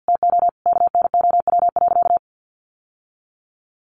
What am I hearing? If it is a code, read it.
Morse code, heard as a keyed tone, about 50 wpm, 700 Hz. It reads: TO KNOW.
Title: Sentences from Top 1200 Words 50wpm